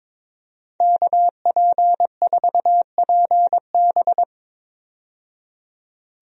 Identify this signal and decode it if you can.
Morse code, heard as a keyed tone, about 21 wpm, 700 Hz. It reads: KP4PB
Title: DXCC-100 Call Signs - 5 Characters - 22wpm